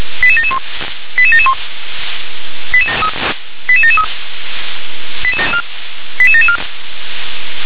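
Five-tone selective call (ZVEI), heard as quick runs of five tones, tones of about 70 ms. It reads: 88771 88771 88773 88774
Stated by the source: ZVEI five-tone call 88742 received by SDR.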